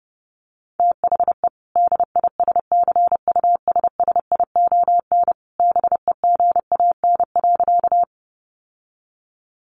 Morse code, heard as a keyed tone, about 30 wpm, 700 Hz. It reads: THE DISCUSSION BEGAN.
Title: Sentences from Top 1100 Words 30wpm